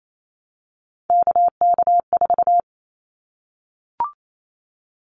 Morse code, heard as a keyed tone, about 28 wpm, 700 Hz. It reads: KK4
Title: Callsign Prefixes for Top 25 DXCC - Repeated 3x - 28wpm